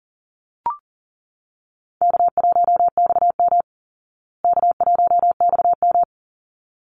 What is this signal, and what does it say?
Morse code, heard as a keyed tone, about 40 wpm, 700 Hz. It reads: K1XM K1XM
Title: Call Signs Repeated 3x - 40wpm